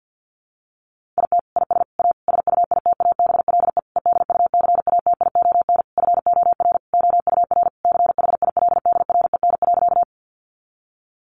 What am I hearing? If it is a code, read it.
Morse code, heard as a keyed tone, about 50 wpm, 700 Hz. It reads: IT IS A SUITABLE EDUCATION FOR OUR CHILDREN.
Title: Sentences from Top 1400 Words 50wpm